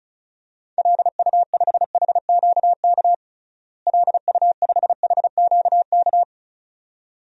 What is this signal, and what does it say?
Morse code, heard as a keyed tone, about 36 wpm, 700 Hz. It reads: LU5HQK LU5HQK
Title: DXCC-100 Call Signs - 6 Characters - Repeated 2x - 35wpm